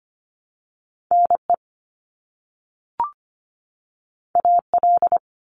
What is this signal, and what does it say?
Morse code, heard as a keyed tone, about 25 wpm, 700 Hz. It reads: NE AL
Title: US States 25wpm